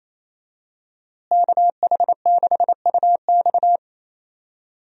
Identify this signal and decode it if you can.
Morse code, heard as a keyed tone, about 28 wpm, 700 Hz. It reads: KH6UX
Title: USA Call Signs - 5 Characters - Repeated 3x - 28wpm